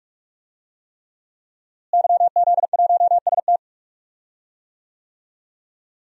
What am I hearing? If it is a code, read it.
Morse code, heard as a keyed tone, about 46 wpm, 700 Hz. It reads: YZ1ST
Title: DXCC-100 Call Signs - 5 Characters - Repeated 3x - 45wpm